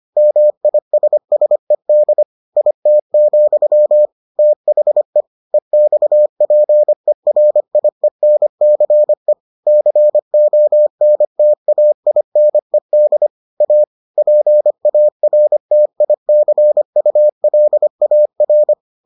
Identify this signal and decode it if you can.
Morse code, heard as a keyed tone, about 25 wpm, 600 Hz. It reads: MISSED IT, THE EXPERIENCE CONTAINED A PARTICULAR